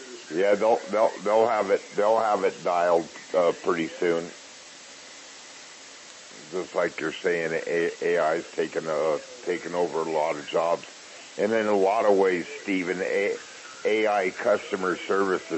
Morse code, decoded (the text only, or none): none